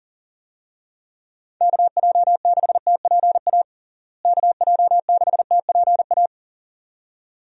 Morse code, heard as a keyed tone, about 40 wpm, 700 Hz. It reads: KJ6TPA KJ6TPA
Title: USA Call Signs - 6 Characters - Repeated 2x - 40wpm